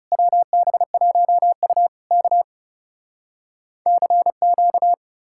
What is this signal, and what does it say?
Morse code, heard as a keyed tone, about 37 wpm, 700 Hz. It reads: WB1U K CQ
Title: Speed-Racing Common QSO Phrases - 30wpm